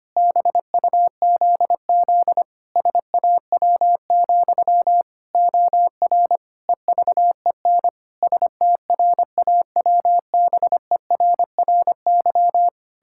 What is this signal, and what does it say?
Morse code, heard as a keyed tone, about 25 wpm, 700 Hz. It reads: BUZZ SAW, OR EVEN STRAWBERRY
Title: Pangram Challenge v2 - 25wpm